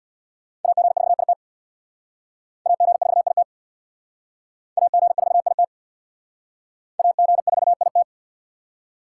Morse code, heard as a keyed tone, about 66 wpm, 700 Hz. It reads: AG4IT AG4IT AG4IT AG4IT